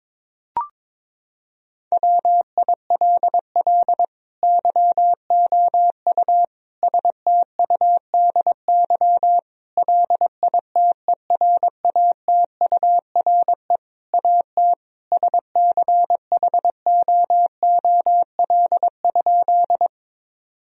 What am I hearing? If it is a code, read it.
Morse code, heard as a keyed tone, about 22 wpm, 700 Hz. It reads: WILL YOU STUDY LITERATURE AT SCHOOL?
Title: Sentences from Top 1000 Words - Review Part 3 - 22wpm